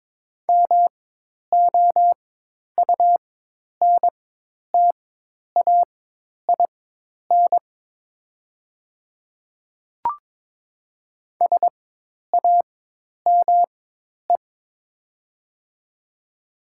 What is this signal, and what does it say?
Morse code, heard as a keyed tone, about 22 wpm, spaced as for 10 wpm, 700 Hz. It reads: MOUNTAIN SAME
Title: 4x Character Spacing - Top 500 Words - 22wpm